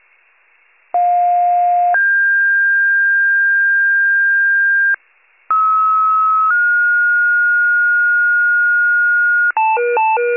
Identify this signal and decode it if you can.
Two-tone page, heard as a one-second tone, then a three-second tone, about 710 then 1600 Hz, about 1290 then 1430 Hz, with noise. It is heard from 0.9 to 9.5 s.
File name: pager / 46 RESCUE DUTY / wcalert